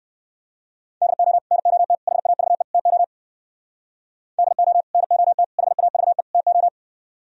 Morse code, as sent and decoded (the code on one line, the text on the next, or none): -.. --- / -. --- - / .... .- ...- . / - --- / -.. --- / -. --- - / .... .- ...- . / - ---
DO NOT HAVE TO DO NOT HAVE TO